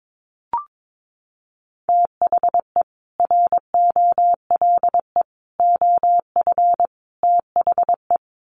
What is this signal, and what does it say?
Morse code, heard as a keyed tone, about 22 wpm, 700 Hz. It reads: THE ROLE OF THE
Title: Sets of 4 Words 22wpm